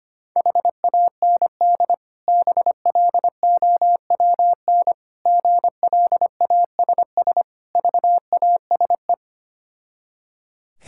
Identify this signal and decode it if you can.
Morse code, heard as a keyed tone, about 25 wpm, 690 Hz. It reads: HAND BLOWN GLASS VASE
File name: practice / 25wpm / Difficult Words with Hints - 25wpm